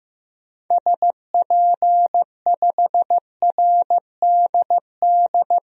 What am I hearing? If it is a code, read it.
Morse code, heard as a keyed tone, about 15 wpm, 700 Hz. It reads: SP5RDD
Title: DXCC-100 Call Signs - 6 Characters - Repeated 2x - 15wpm